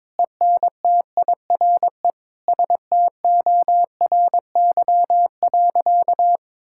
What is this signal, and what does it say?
Morse code, heard as a keyed tone, about 22 wpm, 700 Hz. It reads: ENTIRE STORY.
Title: Sentences from Top 1000 Words - Review Part 5 - 22wpm